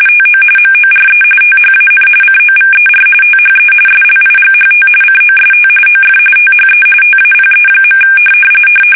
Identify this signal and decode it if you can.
RTTY, about 45.45 baud. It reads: TO MORE THAN COVER THE SLIGHT COST OF THE TAPE.
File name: fsk850